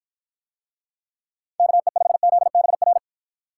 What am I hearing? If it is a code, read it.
Morse code, heard as a keyed tone, about 53 wpm, 700 Hz. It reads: KE5ZBR